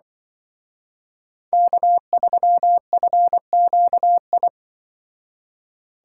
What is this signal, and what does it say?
Morse code, heard as a keyed tone, about 24 wpm, 700 Hz. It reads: K3FQI
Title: Speed Racing - USA 1x3 Call Signs 20wpm